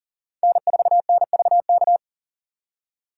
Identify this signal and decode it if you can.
Morse code, heard as a keyed tone, about 51 wpm, 700 Hz. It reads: N4NVK